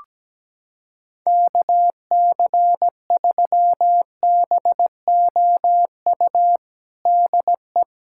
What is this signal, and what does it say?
Morse code, heard as a keyed tone, about 17 wpm, 700 Hz. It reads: KC3BOU DE
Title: USA Callsign DE Callsign 17wpm